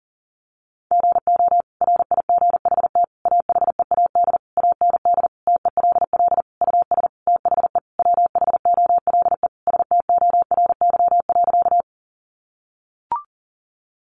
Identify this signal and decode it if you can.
Morse code, heard as a keyed tone, about 40 wpm, 700 Hz. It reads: GO RIGHT AHEAD AND TELL US THE WHOLE STORY.
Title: Sentences from Top 1100 Words 40wpm